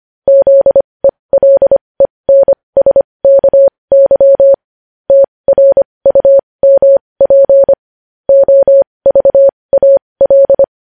Morse code, heard as a keyed tone, about 25 wpm, spaced as for 20 wpm, 550 Hz. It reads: ZELENSKY TRUMP OVAL
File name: podcast / News headlines-1x-25x20